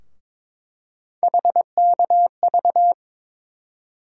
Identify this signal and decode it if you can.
Morse code, heard as a keyed tone, about 22 wpm, 700 Hz. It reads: HKV
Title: USA Call Sign Postfixes - 3 Letters 22wpm